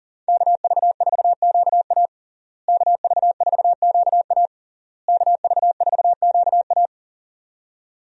Morse code, heard as a keyed tone, about 40 wpm, 700 Hz. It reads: KV4QA KV4QA KV4QA
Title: USA Call Signs - 5 Characters - Repeated 3x - 40wpm